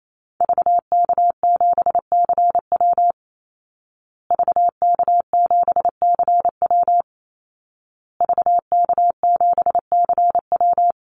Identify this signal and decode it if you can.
Morse code, heard as a keyed tone, about 28 wpm, 700 Hz. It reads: VK7CW VK7CW VK7CW
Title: Call Signs Repeated 2x - 28wpm